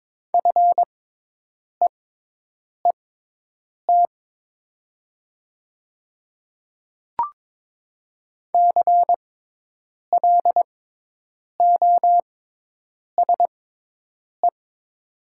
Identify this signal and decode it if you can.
Morse code, heard as a keyed tone, about 22 wpm, spaced as for 8 wpm, 700 Hz. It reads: FEET CLOSE
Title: 6x Character Spacing - Top 300 Words - 22wpm